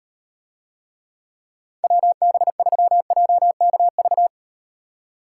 Morse code, heard as a keyed tone, about 38 wpm, 700 Hz. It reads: WB3JKV